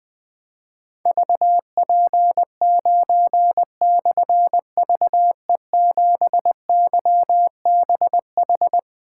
Morse code, heard as a keyed tone, about 20 wpm, 700 Hz. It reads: VP9/VE7YBH
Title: Difficult Call Signs 20wpm